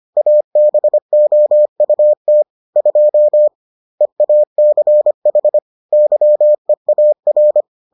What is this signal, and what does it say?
Morse code, heard as a keyed tone, about 25 wpm, 600 Hz. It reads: ABOUT 2 EACH YEAR